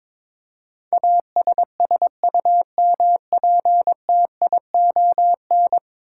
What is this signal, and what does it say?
Morse code, heard as a keyed tone, about 22 wpm, 700 Hz. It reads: ASSUMPTION
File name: practice / 22wpm / 10 Letter Words - First 3 Letters Given - 22wpm